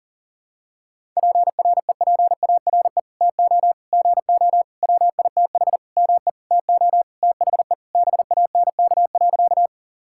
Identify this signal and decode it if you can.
Morse code, heard as a keyed tone, about 40 wpm, 700 Hz. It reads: PREPARE TO GO WITH ME TO THE BANK.
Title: Sentences from Top 1000 Words - Review Part 6 - 40wpm